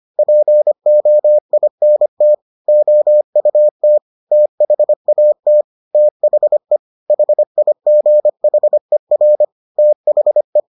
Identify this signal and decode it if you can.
Morse code, heard as a keyed tone, about 25 wpm, 600 Hz. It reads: POINT OUT THAT THE HIGHER THE